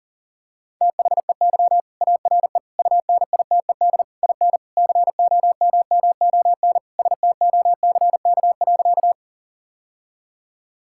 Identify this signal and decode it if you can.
Morse code, heard as a keyed tone, about 40 wpm, 700 Hz. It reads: THEY ARE UNITED IN COMMON STOCK.
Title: Sentences from Top 1000 Words - Review Part 5 - 40wpm